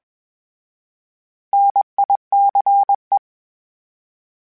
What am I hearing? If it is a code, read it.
Morse code, heard as a keyed tone, about 21 wpm, 800 Hz. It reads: NICE
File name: Misc / Misc-mp3s / Misc @20